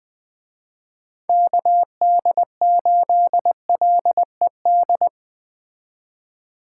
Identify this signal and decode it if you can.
Morse code, heard as a keyed tone, about 20 wpm, 700 Hz. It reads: KD8LED
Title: Speed Racing - USA 2x3 Call Signs 17wpm